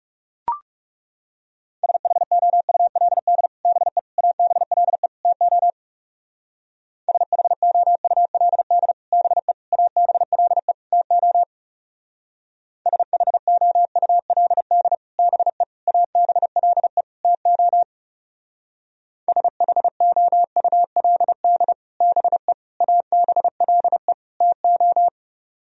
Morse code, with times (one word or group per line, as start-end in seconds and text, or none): 1.8-3.5 SHOULD
3.7-4.0 BE
4.2-5.1 ABLE
5.2-5.7 TO
7.1-8.9 SHOULD
9.1-9.5 BE
9.7-10.7 ABLE
10.9-11.4 TO
12.9-14.9 SHOULD
15.2-15.6 BE
15.9-17.0 ABLE
17.2-17.8 TO
19.3-21.7 SHOULD
22.0-22.5 BE
22.8-24.1 ABLE
24.4-25.1 TO